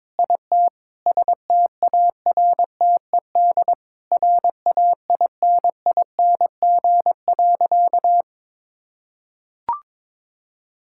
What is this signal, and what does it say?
Morse code, heard as a keyed tone, about 22 wpm, 700 Hz. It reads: IT STARTED RAINING.
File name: practice / 22wpm / Sentences with 4-Word Phrases - 22wpm